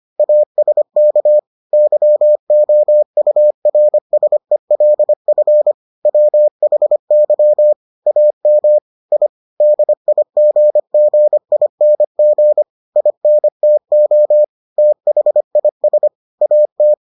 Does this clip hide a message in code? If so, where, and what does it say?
Yes: Morse code from 0.2 to 16.9 s, heard as a keyed tone, about 25 wpm, 600 Hz. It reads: ASK YOURSELF WHY AM I DIGGING INTO THIS AT